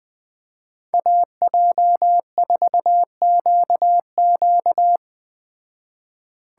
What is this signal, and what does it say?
Morse code, heard as a keyed tone, about 20 wpm, 700 Hz. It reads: AJ4QQ
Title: Speed Racing - USA 2x2 Call Signs 17wpm